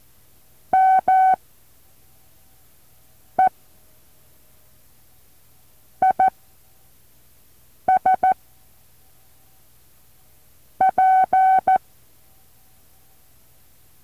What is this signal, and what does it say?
Morse code, heard as a keyed tone, about 14 wpm, spaced as for 4 wpm, 760 Hz. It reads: ME IS P
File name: intl morse code2